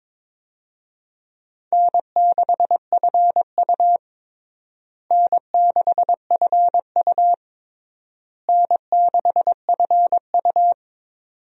Morse code, heard as a keyed tone, about 22 wpm, 700 Hz. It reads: N6FU N6FU N6FU
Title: USA Call Signs - 4 Characters - Repeated 3x - 22wpm